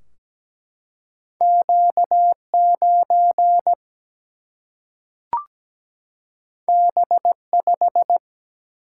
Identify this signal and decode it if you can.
Morse code, heard as a keyed tone, about 17 wpm, 700 Hz. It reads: Q9 B5
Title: Single-Letter Single-Number 17wpm